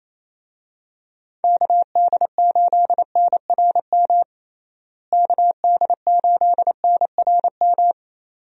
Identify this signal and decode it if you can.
Morse code, heard as a keyed tone, about 28 wpm, 700 Hz. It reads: KD8NRM KD8NRM
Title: USA Call Signs - 6 Characters - Repeated 2x - 28wpm